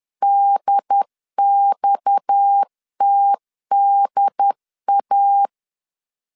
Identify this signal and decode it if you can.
Morse code, heard as a keyed tone, about 10 wpm, 800 Hz. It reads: DXTDA
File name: CW / DET / 1 6 1500C S C 10wpm 4252